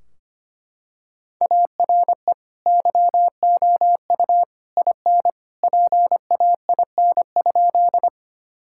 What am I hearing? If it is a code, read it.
Morse code, heard as a keyed tone, about 25 wpm, 700 Hz. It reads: ARE YOU IN PAIN?
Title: Sentences from Top 1600 Words 25wpm